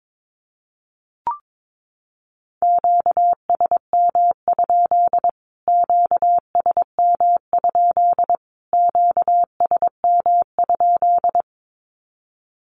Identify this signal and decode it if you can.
Morse code, heard as a keyed tone, about 22 wpm, 700 Hz. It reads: QSM? QSM? QSM?
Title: Q-Codes Repeated 3x 22wpm